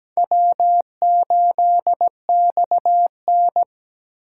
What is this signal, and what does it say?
Morse code, heard as a keyed tone, about 17 wpm, 700 Hz. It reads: W8XN
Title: USA Call Signs - 4 Characters - 2x Character Spacing - 17wpm